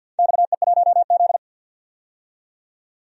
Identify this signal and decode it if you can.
Morse code, heard as a keyed tone, about 52 wpm, 700 Hz. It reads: XE1Z
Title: DXCC-100 Call Signs - 4 Characters - 50wpm